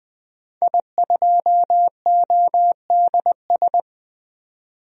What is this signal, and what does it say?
Morse code, heard as a keyed tone, about 20 wpm, 700 Hz. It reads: I2ODS
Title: DXCC-100 Call Signs - 5 Characters - 20wpm